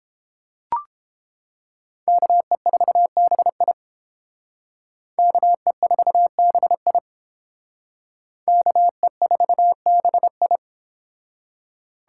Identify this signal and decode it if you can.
Morse code, heard as a keyed tone, about 31 wpm, 700 Hz. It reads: KE4BI KE4BI KE4BI